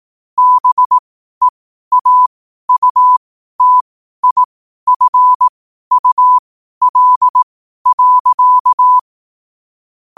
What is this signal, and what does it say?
Morse code, heard as a keyed tone, about 18 wpm, spaced as for 12 wpm, 1000 Hz. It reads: BEAUTIFUL.
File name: qotdfiles / 12wpm